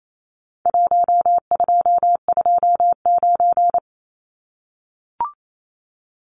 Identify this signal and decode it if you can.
Morse code, heard as a keyed tone, about 28 wpm, 700 Hz. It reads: 1229
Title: Numbers - Contest Style - 4-Digit 28wpm